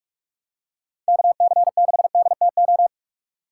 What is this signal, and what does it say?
Morse code, heard as a keyed tone, about 44 wpm, 700 Hz. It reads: KC6DTO